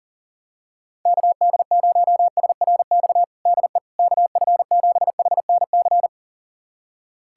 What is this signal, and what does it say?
Morse code, heard as a keyed tone, about 37 wpm, 700 Hz. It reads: KD0SRX DE KF7HNC
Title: USA Callsign DE Callsign - 6 Characters - 40wpm